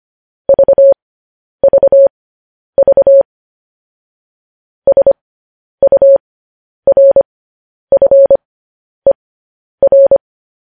Morse code, heard as a keyed tone, about 25 wpm, spaced as for 10 wpm, 550 Hz. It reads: VVV SURFER
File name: podcast / News headlines-1x-25x10